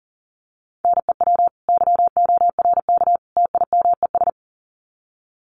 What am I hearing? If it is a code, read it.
Morse code, heard as a keyed tone, about 40 wpm, 700 Hz. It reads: NEW YORK TIMES
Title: Sets of 3 Words 40wpm